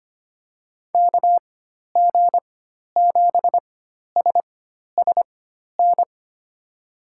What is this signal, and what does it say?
Morse code, heard as a keyed tone, about 25 wpm, spaced as for 12 wpm, 700 Hz. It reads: KG7SSN
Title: USA Call Signs - 6 Characters - 4x Character Spacing - 25wpm